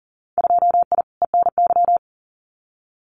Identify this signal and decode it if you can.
Morse code, heard as a keyed tone, about 39 wpm, 700 Hz. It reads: 2I ENY
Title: Winter Field Day Exchanges Encore 40wpm